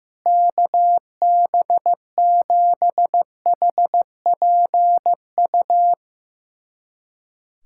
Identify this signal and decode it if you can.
Morse code, heard as a keyed tone, about 15 wpm, 700 Hz. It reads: KB7HPU
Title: USA Callsign DE Callsign - 6 Characters - 15wpm